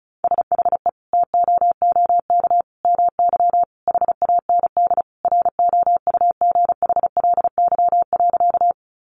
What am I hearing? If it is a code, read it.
Morse code, heard as a keyed tone, about 34 wpm, 700 Hz. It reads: SHE TOOK MY HAND ROUGHLY.
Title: Sentences from Top 1500 Words 35wpm